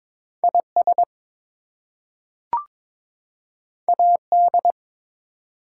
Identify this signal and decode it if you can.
Morse code, heard as a keyed tone, about 22 wpm, 700 Hz. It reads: IS AD
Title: Two Letter Words - 22wpm